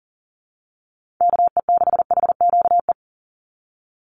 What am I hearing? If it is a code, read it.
Morse code, heard as a keyed tone, about 42 wpm, 700 Hz. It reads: KE6HQE
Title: USA Call Signs 40wpm